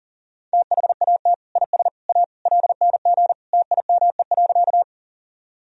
Morse code, heard as a keyed tone, about 40 wpm, 700 Hz. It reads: THAT IS A LNG TIME.
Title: Sentences from Top 100 Words and Abbreviations 40wpm